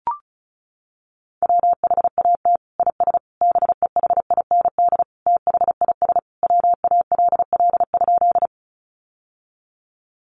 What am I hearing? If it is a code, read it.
Morse code, heard as a keyed tone, about 34 wpm, 700 Hz. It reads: WHAT IS BEHIND THIS WALL?
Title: Sentences from Top 1100 Words 35wpm